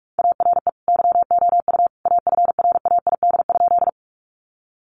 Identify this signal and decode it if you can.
Morse code, heard as a keyed tone, about 44 wpm, 700 Hz. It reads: ARE YOU AFRAID?
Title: Sentences from Top 1500 Words 45wpm